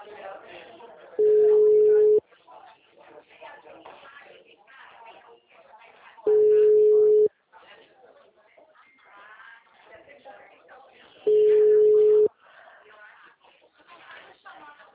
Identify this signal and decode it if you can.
Ringback tone (Europe, 425 Hz), with speech.